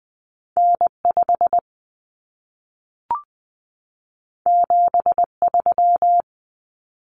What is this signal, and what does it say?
Morse code, heard as a keyed tone, about 20 wpm, 700 Hz. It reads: N5 73
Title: Common Cut Numbers - 2-Digit 20wpm